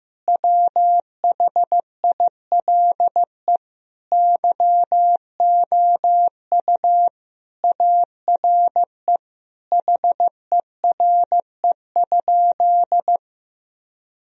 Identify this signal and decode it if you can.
Morse code, heard as a keyed tone, about 15 wpm, 700 Hz. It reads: WHILE YOU ARE HERE?
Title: Sentences from Top 1000 Words - Review Part 1 - 15wpm